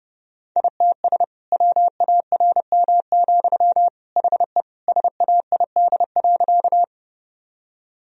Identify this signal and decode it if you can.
Morse code, heard as a keyed tone, about 32 wpm, 700 Hz. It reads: ITS WARM, HE SAID.
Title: Sentences from Top 1000 Words - Review Part 2 - 30wpm